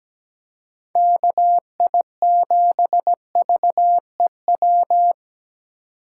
Morse code, heard as a keyed tone, about 17 wpm, 700 Hz. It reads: KI7VEW
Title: USA Call Signs - 6 Characters - 2x Character Spacing - 17wpm